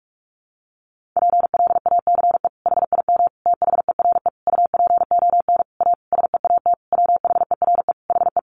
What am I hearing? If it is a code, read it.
Morse code, heard as a keyed tone, about 44 wpm, 700 Hz. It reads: PLACE HIM THERE UPON A SEAT WHERE HE
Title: Sentences from Top 1500 Words 45wpm